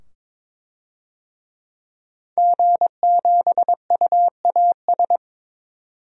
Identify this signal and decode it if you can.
Morse code, heard as a keyed tone, about 22 wpm, 700 Hz. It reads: G7UAS
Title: DXCC-100 Call Signs - 5 Characters - 22wpm